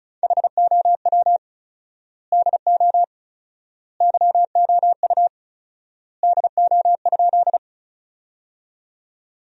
Morse code, heard as a keyed tone, about 36 wpm, 700 Hz. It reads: HOW DO YOU DO?
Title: 4x Word-Spacing - Sentences from Top 100 Words - 35wpm